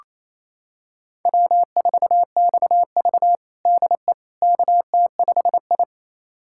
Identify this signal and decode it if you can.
Morse code, heard as a keyed tone, about 28 wpm, 700 Hz. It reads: W4XV DE KT5I
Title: USA Callsign DE Callsign - 4 Characters - 28wpm